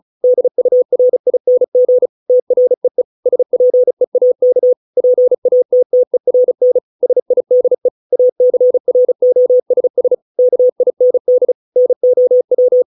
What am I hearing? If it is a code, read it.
Morse code, heard as a keyed tone, about 34 wpm, 500 Hz. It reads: DURING TREE SPEAK PATTERN SIDE ACROSS KIND NOW